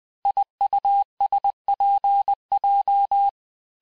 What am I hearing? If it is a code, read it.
Morse code, heard as a keyed tone, about 20 wpm, 800 Hz. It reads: IUSPJ
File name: cours-858-texte-chiffre-vitesse-19mpm